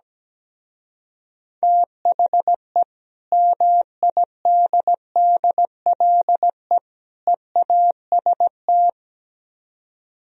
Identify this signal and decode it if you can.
Morse code, heard as a keyed tone, about 17 wpm, 700 Hz. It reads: THE MIDDLE EAST
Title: Speed-Racing - Sets of 3 Words 17wpm